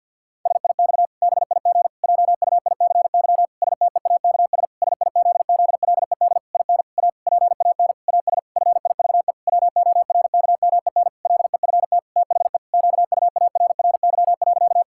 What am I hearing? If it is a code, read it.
Morse code, heard as a keyed tone, about 51 wpm, 700 Hz. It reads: SIX BIG JUICY STEAKS SIZZLED IN A PAN AS FIVE WORKMEN LEFT THE QUARRY.
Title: Pangram Challenge v2 - 50wpm